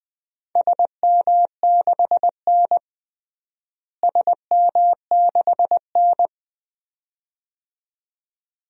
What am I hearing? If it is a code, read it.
Morse code, heard as a keyed tone, about 20 wpm, 700 Hz. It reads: SM6N SM6N
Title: DXCC-100 Call Signs - 4 Characters - Repeated 2x - 20wpm